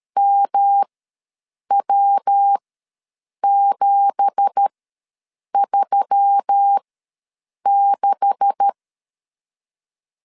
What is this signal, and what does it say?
Morse code, heard as a keyed tone, about 13 wpm, spaced as for 7 wpm, 800 Hz. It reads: MW736
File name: LEZIONI / 9 2867 ws 12wpm 3431